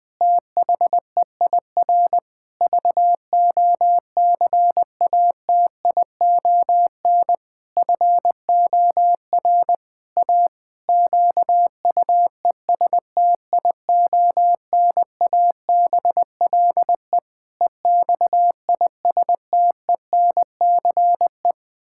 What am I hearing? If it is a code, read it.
Morse code, heard as a keyed tone, about 21 wpm, 700 Hz. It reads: THEIR VOCATION FOR A QUESTIONABLE EXISTENCE
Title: Pangram Challenge v2 - 20wpm